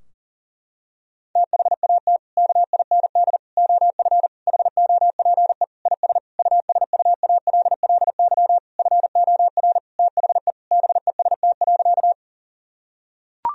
Morse code, heard as a keyed tone, about 40 wpm, 700 Hz. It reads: THAT KIND OF HOPE IS USUALLY FOR THE BEST.
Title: Sentences from Top 1000 Words - Review Part 3 - 40wpm